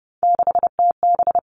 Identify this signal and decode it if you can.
Morse code, heard as a keyed tone, about 30 wpm, 700 Hz. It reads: 6TB
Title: All Cut Numbers - 3-Digit 30wpm